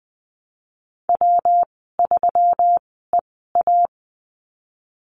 Morse code, heard as a keyed tone, about 20 wpm, spaced as for 15 wpm, 700 Hz. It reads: W3EA
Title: 122 Lesson 20g - Number 3 - Call Signs 20wpm